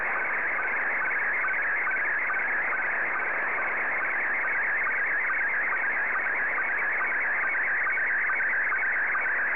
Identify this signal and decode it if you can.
SSTV (slow-scan television), Scottie 1.